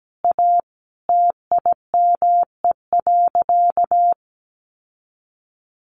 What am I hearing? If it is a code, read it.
Morse code, heard as a keyed tone, about 17 wpm, 700 Hz. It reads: A TIME.
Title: Sentences from Top 1200 Words 17wpm